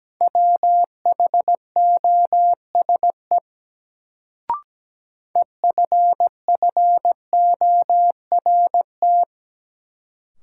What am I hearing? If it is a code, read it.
Morse code, heard as a keyed tone, about 17 wpm, 700 Hz. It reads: WHOSE EFFORT
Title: Sentences from Top 1200 Words - New Word Drill Alt - 17wpm